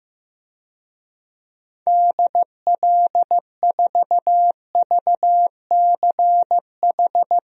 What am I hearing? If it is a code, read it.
Morse code, heard as a keyed tone, about 15 wpm, 700 Hz. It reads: DL4VCH
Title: DXCC-100 Call Signs - 6 Characters - Repeated 2x - 15wpm